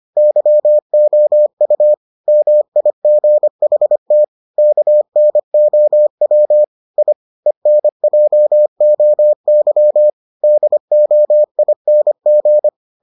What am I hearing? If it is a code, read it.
Morse code, heard as a keyed tone, about 25 wpm, 600 Hz. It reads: YOU MIGHT KNOW I ENJOY DOING